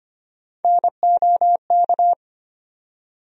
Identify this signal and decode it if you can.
Morse code, heard as a keyed tone, about 25 wpm, 700 Hz. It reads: NOK
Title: USA Call Sign Postfixes - 3 Letters 25wpm